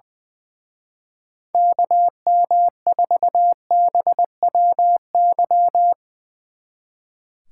Morse code, heard as a keyed tone, about 21 wpm, 700 Hz. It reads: KM4BWY
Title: Speed Racing - USA 2x3 Call Signs 20wpm